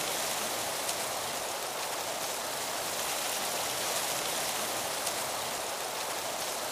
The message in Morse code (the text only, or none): none